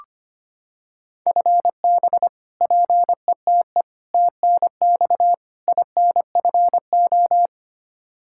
Morse code, heard as a keyed tone, about 25 wpm, 700 Hz. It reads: FB PETE TNX INFO
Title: Practice QSO - Exchange 3 - More Info - 25wpm